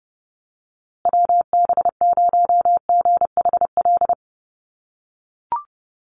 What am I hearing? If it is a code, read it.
Morse code, heard as a keyed tone, about 30 wpm, 700 Hz. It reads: WB0GHL
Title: USA Call Signs Repeated 2x 30wpm